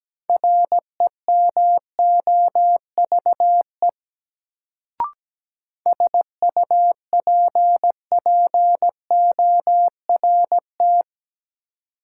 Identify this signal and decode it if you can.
Morse code, heard as a keyed tone, about 17 wpm, 700 Hz. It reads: REMOVE SUPPORT
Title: Sentences from Top 800 Words - New Word Drill - 17wpm